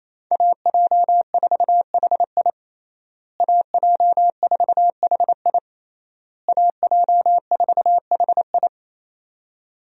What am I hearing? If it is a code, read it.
Morse code, heard as a keyed tone, about 27 wpm, 700 Hz. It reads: AJ4HI AJ4HI AJ4HI
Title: USA Call Signs - 5 Characters - Repeated 3x - 28wpm